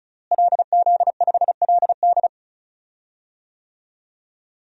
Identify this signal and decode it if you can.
Morse code, heard as a keyed tone, about 36 wpm, 700 Hz. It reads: LZ5LD
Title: DXCC-100 Call Signs - 5 Characters - Repeated 2x - 35wpm